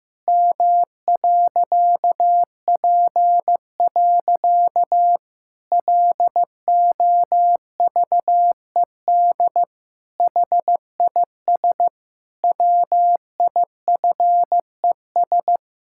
Morse code, heard as a keyed tone, about 14 wpm, 700 Hz. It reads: M.P. LOVED HIS WIFES
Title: Pangram Challenge v2 - 15wpm